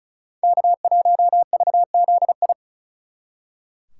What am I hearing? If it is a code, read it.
Morse code, heard as a keyed tone, about 46 wpm, 700 Hz. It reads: K1VZI